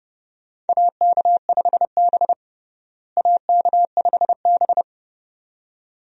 Morse code, heard as a keyed tone, about 30 wpm, 700 Hz. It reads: AK5B AK5B
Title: USA Call Signs - 4 Characters - Repeated 2x - 30wpm